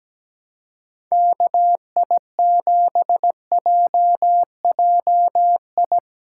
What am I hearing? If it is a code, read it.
Morse code, heard as a keyed tone, about 17 wpm, 700 Hz. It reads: KI7JJI
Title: USA Call Signs - 6 Characters - 4x Character Spacing - 17wpm